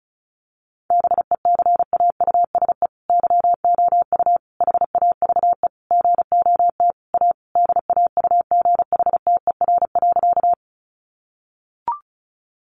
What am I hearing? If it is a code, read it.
Morse code, heard as a keyed tone, about 34 wpm, 700 Hz. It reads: BECAUSE YOU HAVE GOT A DAUGHTER.
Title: Sentences from Top 1500 Words 35wpm